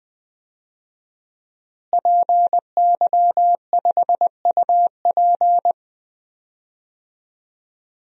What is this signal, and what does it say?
Morse code, heard as a keyed tone, about 20 wpm, 700 Hz. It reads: PY5UP
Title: DXCC-100 Call Signs - 5 Characters - Repeated 3x - 20wpm